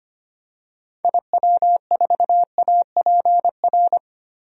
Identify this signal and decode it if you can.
Morse code, heard as a keyed tone, about 25 wpm, 690 Hz. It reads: IW4APR
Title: DXCC-100 Call Signs - 6 Characters - 25wpm